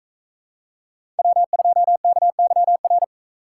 Morse code, heard as a keyed tone, about 44 wpm, 700 Hz. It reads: W2KYR